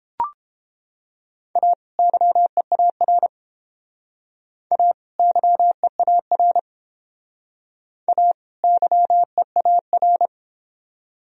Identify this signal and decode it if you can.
Morse code, heard as a keyed tone, about 30 wpm, 700 Hz. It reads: A YEAR A YEAR A YEAR